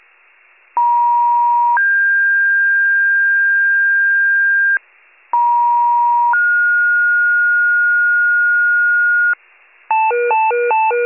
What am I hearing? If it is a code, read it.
Two-tone page, heard as a one-second tone, then a three-second tone, about 950 then 1600 Hz, about 950 then 1430 Hz, with noise. It is heard from 0.8 to 9.3 s.